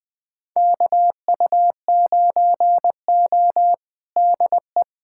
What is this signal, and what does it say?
Morse code, heard as a keyed tone, about 20 wpm, 700 Hz. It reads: KU9O DE
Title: USA Callsign DE Callsign - 4 Characters - 20wpm